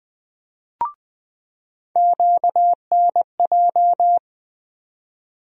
Morse code, heard as a keyed tone, about 21 wpm, 700 Hz. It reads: QNJ
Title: USA Call Sign Postfixes - 3 Letters 20wpm